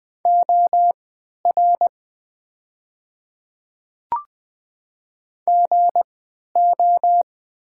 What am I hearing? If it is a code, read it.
Morse code, heard as a keyed tone, about 20 wpm, spaced as for 11 wpm, 700 Hz. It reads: OR GO
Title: 3x Character Spacing - Top 100 Words - 20wpm